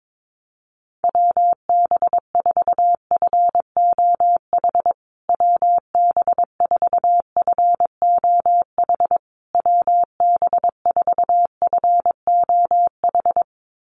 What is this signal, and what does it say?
Morse code, heard as a keyed tone, about 22 wpm, 700 Hz. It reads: WB4FOH WB4FOH WB4FOH
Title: USA Call Signs Repeated 3x 22wpm